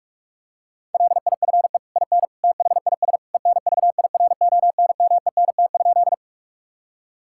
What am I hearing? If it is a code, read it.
Morse code, heard as a keyed tone, about 46 wpm, 700 Hz. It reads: LIFE IN THIS ENVIRONMENT?
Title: Sentences from Top 1000 Words - Review Part 5 - 45wpm